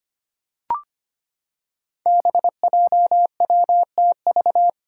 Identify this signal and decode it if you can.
Morse code, heard as a keyed tone, about 25 wpm, 700 Hz. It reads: BJWTV
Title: Five Challenging Letter-Number Combinations 25wpm